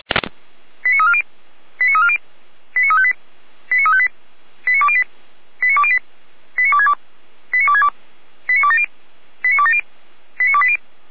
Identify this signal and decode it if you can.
Five-tone selective call (ZVEI), heard as quick runs of five tones, tones of about 70 ms. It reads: 89340 89340 89358 89358 89208 89208 89262 89262 89270 89270 89280
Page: /455/SDR/ZVEI/89110/